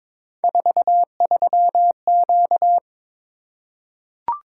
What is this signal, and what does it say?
Morse code, heard as a keyed tone, about 22 wpm, 700 Hz. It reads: 43Q
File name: practice / 22wpm / Three Letter-Number Combinations - 22wpm